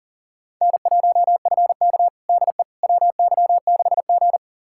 Morse code, heard as a keyed tone, about 37 wpm, 700 Hz. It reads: N1FK DE WY6G
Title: USA Callsign DE Callsign - 4 Characters - 40wpm